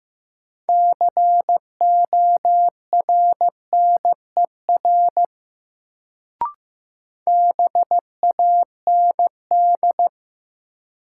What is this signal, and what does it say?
Morse code, heard as a keyed tone, about 15 wpm, 700 Hz. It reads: CORNER BAND
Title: Sentences from Top 1600 Words - New Word Drill Alt - 15wpm